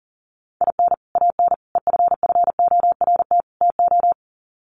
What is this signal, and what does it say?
Morse code, heard as a keyed tone, about 40 wpm, 700 Hz. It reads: IN AN EFFORT TO
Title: Sets of 4 Words 40wpm